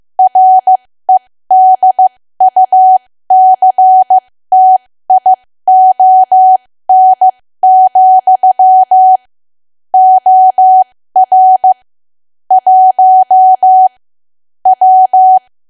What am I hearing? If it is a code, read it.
Morse code, heard as a keyed tone, about 15 wpm, spaced as for 13 wpm, 750 Hz. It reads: REDUCTION, OR 1 W
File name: cwa / W1AW-Level2 / 13WPM